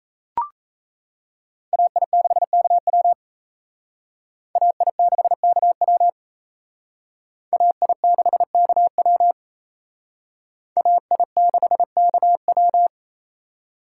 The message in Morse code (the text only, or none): AI6KW AI6KW AI6KW AI6KW